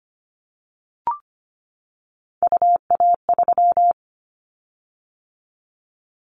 Morse code, heard as a keyed tone, about 25 wpm, 700 Hz. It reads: UA3
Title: Callsign Prefixes for Top 50 DXCC - 25wpm